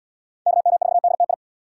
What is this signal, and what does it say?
Morse code, heard as a keyed tone, about 75 wpm, 700 Hz. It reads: KM4DTI